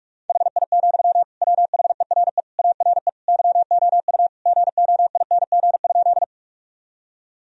Morse code, heard as a keyed tone, about 47 wpm, 700 Hz. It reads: HI, WHERE ARE YOU GOING?
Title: Sentences from Top 1000 Words - Review Part 3 - 45wpm